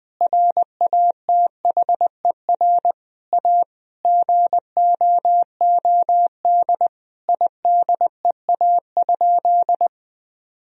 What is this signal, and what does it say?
Morse code, heard as a keyed tone, about 20 wpm, 700 Hz. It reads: RATHER A GOOD IDEA?
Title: Sentences from Top 1000 Words - Review Part 4 - 20wpm